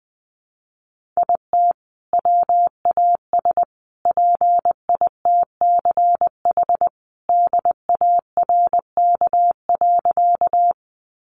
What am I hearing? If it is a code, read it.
Morse code, heard as a keyed tone, about 19 wpm, 700 Hz. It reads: IT WAS PITCH DARK.
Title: Sentences from Top 1300 Words 20wpm